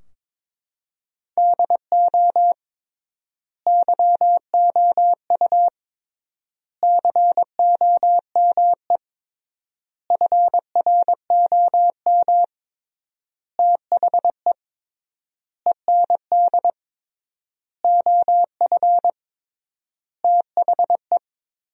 Morse code, heard as a keyed tone, about 22 wpm, 700 Hz. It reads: DO YOU COME FROM THE END OF THE
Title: 3x Word-Spacing - Sentences from Top 200 Words - 22wpm